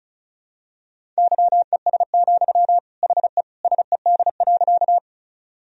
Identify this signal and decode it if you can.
Morse code, heard as a keyed tone, about 36 wpm, 700 Hz. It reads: YES, HE SED.
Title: Sentences from Top 500 Words and Abbreviations 35wpm